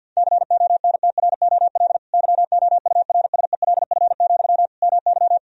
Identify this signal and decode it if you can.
Morse code, heard as a keyed tone, about 50 wpm, 700 Hz. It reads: CONTROL YOURSELF, MY
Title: Sentences from Top 1000 Words - Review Part 3 - 50wpm